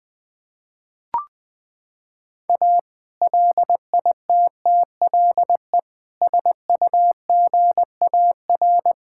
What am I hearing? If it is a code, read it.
Morse code, heard as a keyed tone, about 20 wpm, 700 Hz. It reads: A LITTLE SUGAR
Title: Sentences with 2-Word Phrases - 20wpm